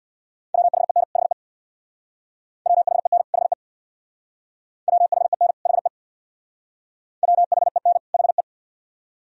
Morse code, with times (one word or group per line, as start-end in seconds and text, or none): 0.5-1.0 WHEN
1.2-1.3 HE
2.7-3.2 WHEN
3.3-3.5 HE
4.9-5.5 WHEN
5.7-5.9 HE
7.2-8.0 WHEN
8.1-8.4 HE